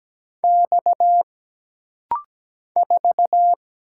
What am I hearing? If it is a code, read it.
Morse code, heard as a keyed tone, about 17 wpm, 700 Hz. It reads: X 4
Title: Single Letter-Number Alt - Rapid-Fire 17wpm